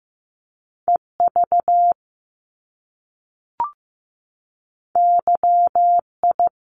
Morse code, heard as a keyed tone, about 15 wpm, 700 Hz. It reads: EV YI
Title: Top Two Letter Combinations in Words 15wpm